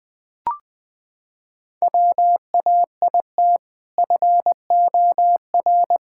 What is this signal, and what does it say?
Morse code, heard as a keyed tone, about 21 wpm, 700 Hz. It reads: WAIT FOR
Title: Sentences with 2-Word Phrases - 20wpm